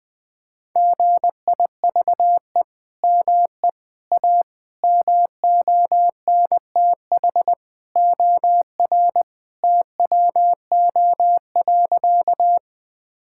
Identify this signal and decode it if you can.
Morse code, heard as a keyed tone, about 20 wpm, 700 Hz. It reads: GIVE ME A MONTH OR TWO.
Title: Sentences from Top 1000 Words - Review Part 5 - 20wpm